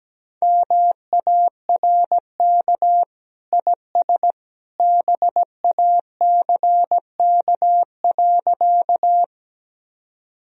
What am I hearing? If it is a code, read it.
Morse code, heard as a keyed tone, about 17 wpm, 700 Hz. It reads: MARK IS BACK.
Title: Sentences from Top 1000 Words - Review Part 1 - 17wpm